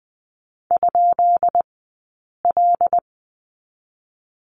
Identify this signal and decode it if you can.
Morse code, heard as a keyed tone, about 20 wpm, spaced as for 8 wpm, 700 Hz. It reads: ?L
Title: 197 Lesson 32d - Letter k - Two Characters 20wpm